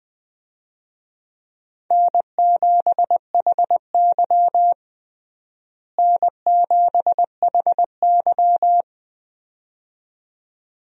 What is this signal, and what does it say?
Morse code, heard as a keyed tone, about 20 wpm, 700 Hz. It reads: N7HY N7HY
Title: USA Call Signs - 4 Characters - Repeated 2x - 20wpm